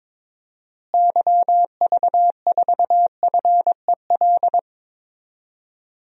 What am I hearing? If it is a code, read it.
Morse code, heard as a keyed tone, about 22 wpm, 700 Hz. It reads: YV4FEL
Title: DXCC-100 Call Signs - 6 Characters - Repeated 3x - 22wpm